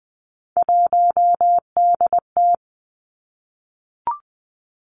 Morse code, heard as a keyed tone, about 20 wpm, 700 Hz. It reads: 1DT
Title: Three Challenging Letter-Number Combinations 20wpm